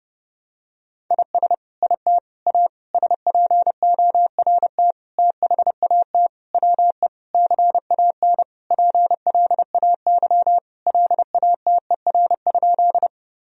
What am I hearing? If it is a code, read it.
Morse code, heard as a keyed tone, about 30 wpm, 700 Hz. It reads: IS IT A SPORT THAT WE CAN PLAY LATER?
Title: Sentences from Top 1000 Words - Review Part 4 - 30wpm